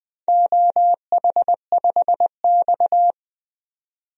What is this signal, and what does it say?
Morse code, heard as a keyed tone, about 20 wpm, 700 Hz. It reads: OH5X
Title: DXCC-100 Call Signs - 4 Characters - 20wpm